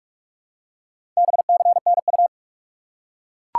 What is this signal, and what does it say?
Morse code, heard as a keyed tone, about 41 wpm, 700 Hz. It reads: BCNU
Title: Top 90 QSO Elements 45wpm